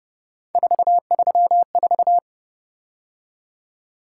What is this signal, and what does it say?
Morse code, heard as a keyed tone, about 29 wpm, 700 Hz. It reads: 434
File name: Speed-Racing - Three Letter-Number Combinations - 25wpm